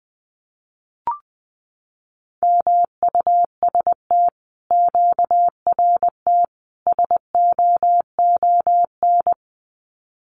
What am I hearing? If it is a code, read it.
Morse code, heard as a keyed tone, about 20 wpm, 700 Hz. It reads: MUST QRT SOON
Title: Common QSO Phrases Encore 20wpm